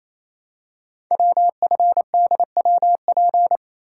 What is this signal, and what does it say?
Morse code, heard as a keyed tone, about 29 wpm, 700 Hz. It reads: WFDWP
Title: Code Groups - 5 Letters - 28wpm